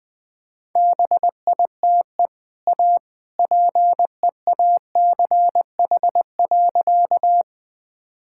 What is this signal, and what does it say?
Morse code, heard as a keyed tone, about 20 wpm, 700 Hz. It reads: BITE A PEACH.
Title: Easy Sentences with Hints - 20wpm